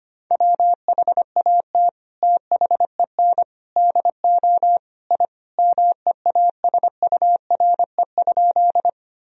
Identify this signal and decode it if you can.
Morse code, heard as a keyed tone, about 25 wpm, 700 Hz. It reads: WHAT THEN DO I MEASURE?
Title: Sentences from Top 1000 Words - Review Part 2 - 25wpm